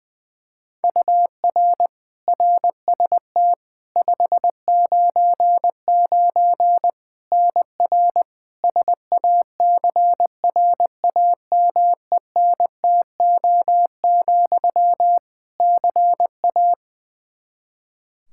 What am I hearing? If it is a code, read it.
Morse code, heard as a keyed tone, about 20 wpm, 700 Hz. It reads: UR RST 599 NR SACRAMENTO, CA
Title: Practice QSO - Exchange 1 - Beginning - 20wpm